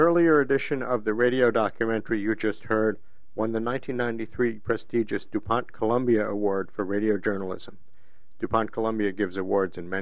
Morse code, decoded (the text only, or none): none